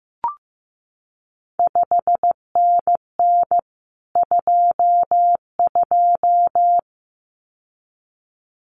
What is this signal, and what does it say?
Morse code, heard as a keyed tone, about 15 wpm, 700 Hz. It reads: 5NN 22
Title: CQ WW DX Contest Exchange Encore - 15wpm